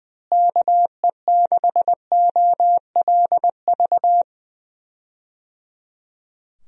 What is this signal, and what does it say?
Morse code, heard as a keyed tone, about 20 wpm, 700 Hz. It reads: KE6OLV
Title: USA Call Signs - 6 Characters - Repeated 3x - 20wpm